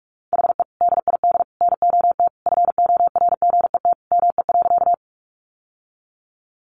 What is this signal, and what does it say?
Morse code, heard as a keyed tone, about 46 wpm, 700 Hz. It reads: HE DID NOT FORGET ME.
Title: Sentences from Top 1200 Words 45wpm